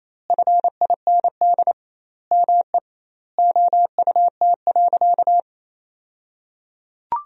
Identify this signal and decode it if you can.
Morse code, heard as a keyed tone, about 28 wpm, 700 Hz. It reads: FIND ME OUT.
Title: 2x Word-Spacing - Sentences from Top 500 Words - 28wpm